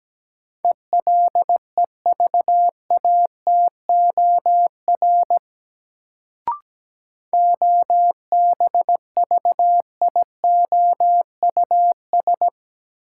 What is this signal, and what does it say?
Morse code, heard as a keyed tone, about 17 wpm, 700 Hz. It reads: ELEVATOR OBVIOUS
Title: Sentences from Top 1600 Words - New Word Drill Alt - 17wpm